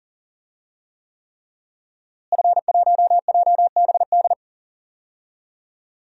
Morse code, heard as a keyed tone, about 37 wpm, 700 Hz. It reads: F1JBD
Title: DXCC-100 Call Signs - 5 Characters - Repeated 3x - 40wpm